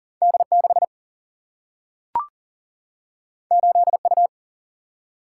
Morse code, heard as a keyed tone, about 39 wpm, 700 Hz. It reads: D6 8U
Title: Single-Letter Single-Number 40wpm